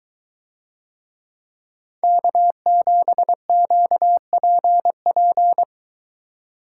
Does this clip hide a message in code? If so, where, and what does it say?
Yes: Morse code from 2.0 to 5.6 s, heard as a keyed tone, about 22 wpm, 700 Hz. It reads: K7QPP